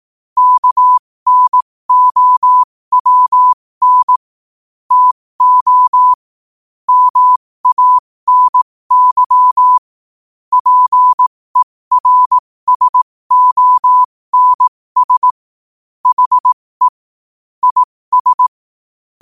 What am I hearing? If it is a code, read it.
Morse code, heard as a keyed tone, about 18 wpm, spaced as for 15 wpm, 1000 Hz. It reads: KNOWN TO MANY PERSONS HE IS
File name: qotdfiles / 15wpm